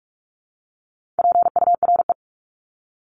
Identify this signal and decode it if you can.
Morse code, heard as a keyed tone, about 44 wpm, 700 Hz. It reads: PURE
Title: LY Suffix - Top 100 Words with Base Words 45wpm